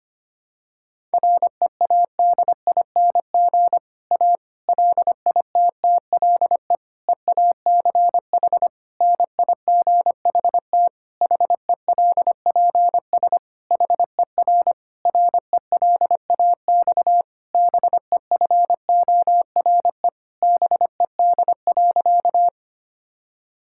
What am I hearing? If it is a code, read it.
Morse code, heard as a keyed tone, about 25 wpm, 700 Hz. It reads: READING A LITTLE EACH NIGHT HELPS HER RELAX BEFORE BED.
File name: practice / 25wpm / Difficult Sentences with Hints - 25wpm